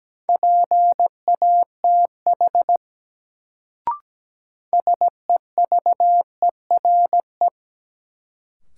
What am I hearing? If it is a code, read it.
Morse code, heard as a keyed tone, about 17 wpm, 700 Hz. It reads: PATH SEVERE
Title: Sentences from Top 1400 Words - New Word Drill Alt - 17wpm